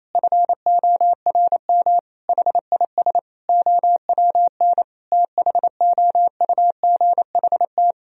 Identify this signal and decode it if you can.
Morse code, heard as a keyed tone, about 28 wpm, 700 Hz. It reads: FORM HIS OWN THOUGHT
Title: Sentences from Top 1000 Words - Review Part 1 - 28wpm